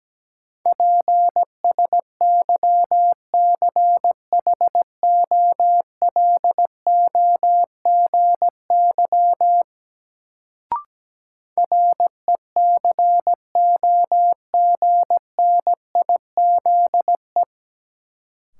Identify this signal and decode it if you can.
Morse code, heard as a keyed tone, about 17 wpm, 700 Hz. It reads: PSYCHOLOGY RECOGNIZE
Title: Sentences from Top 1400 Words - New Word Drill Alt - 17wpm